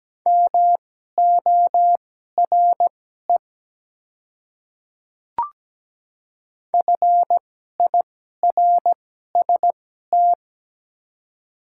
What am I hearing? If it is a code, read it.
Morse code, heard as a keyed tone, about 17 wpm, spaced as for 12 wpm, 700 Hz. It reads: MORE FIRST
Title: 2x Character Spacing - Top 200 Words - 17wpm